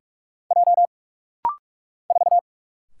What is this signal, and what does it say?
Morse code, heard as a keyed tone, about 44 wpm, 700 Hz. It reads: J 4
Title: Single Letter-Number Alt - Rapid-Fire 45wpm